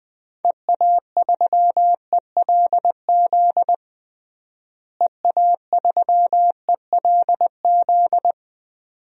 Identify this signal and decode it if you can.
Morse code, heard as a keyed tone, about 20 wpm, 700 Hz. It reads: EA3ELZ EA3ELZ
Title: DXCC-100 Call Signs - 6 Characters - Repeated 2x - 20wpm